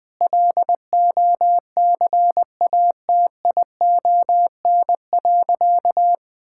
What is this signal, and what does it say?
Morse code, heard as a keyed tone, about 21 wpm, 700 Hz. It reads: LOCATION.
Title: Sentences from Top 1000 Words - Review Part 6 - 20wpm